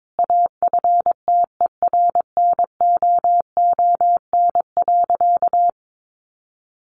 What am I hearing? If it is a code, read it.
Morse code, heard as a keyed tone, about 22 wpm, 700 Hz. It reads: AFTERNOON.
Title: Sentences from Top 1500 Words 22wpm